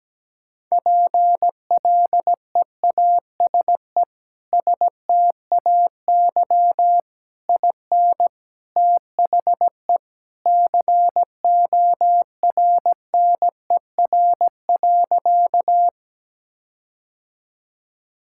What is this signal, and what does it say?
Morse code, heard as a keyed tone, about 17 wpm, 700 Hz. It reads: PLEASE STAY IN THE CORNER.
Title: Sentences from Top 1600 Words 17wpm